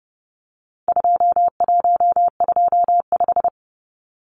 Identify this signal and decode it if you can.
Morse code, heard as a keyed tone, about 30 wpm, 700 Hz. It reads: 2125
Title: Numbers - 4-Digit 30wpm